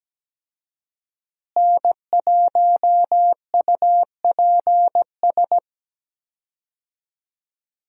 Morse code, heard as a keyed tone, about 17 wpm, 700 Hz. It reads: N1UPS
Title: USA Call Signs - 5 Characters - Repeated 3x - 17wpm